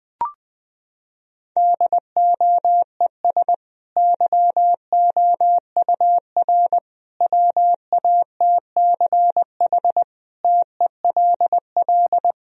Morse code, heard as a keyed tone, about 20 wpm, 700 Hz. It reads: DOES YOUR WATCH TELL
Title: Sentences from Top 1000 Words - Review Part 1 - 20wpm